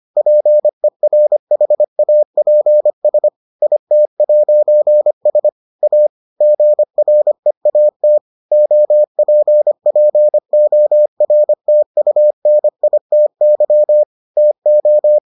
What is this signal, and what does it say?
Morse code, heard as a keyed tone, about 25 wpm, 600 Hz. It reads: PERHAPS IT'S A GREAT OPPORTUNITY TO